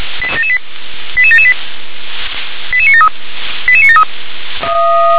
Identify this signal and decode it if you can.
Five-tone selective call (ZVEI), heard as quick runs of five tones, tones of about 70 ms. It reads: 88778 88973 88973
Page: SDR/ZVEI